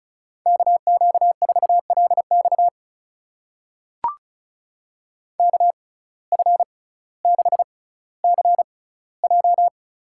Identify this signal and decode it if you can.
Morse code, heard as a keyed tone, about 34 wpm, spaced as for 18 wpm, 700 Hz. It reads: KQ4LX KF6CJ
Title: USA Call Signs - 5 Characters - 6x Character Spacing - 35wpm